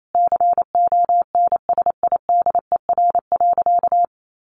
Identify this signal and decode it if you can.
Morse code, heard as a keyed tone, about 28 wpm, 700 Hz. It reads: CONSIDER.
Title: Sentences from Top 900 Words 28wpm